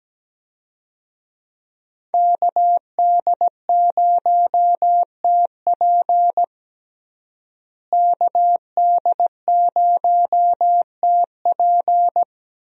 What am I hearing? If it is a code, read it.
Morse code, heard as a keyed tone, about 17 wpm, 700 Hz. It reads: KD0TP KD0TP
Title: USA Call Signs - 5 Characters - Repeated 3x - 17wpm